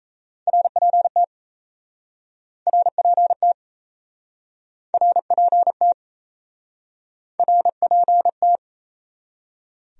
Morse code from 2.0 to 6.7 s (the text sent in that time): RPT RPT